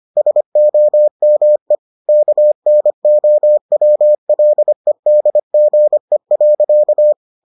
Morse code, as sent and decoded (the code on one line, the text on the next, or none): ... --- -- . / -.- -. --- .-- .-.. -.. --. .-.-.-
SOME KNOWLEDGE.